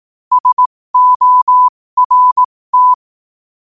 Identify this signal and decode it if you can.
Morse code, heard as a keyed tone, about 18 wpm, spaced as for 15 wpm, 1000 Hz. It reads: SORT
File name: qotdfiles / 15wpm